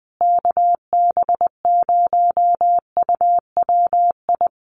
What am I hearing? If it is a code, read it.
Morse code, heard as a keyed tone, about 20 wpm, 700 Hz. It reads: KB0UWI
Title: USA Call Signs Repeated 2x 20wpm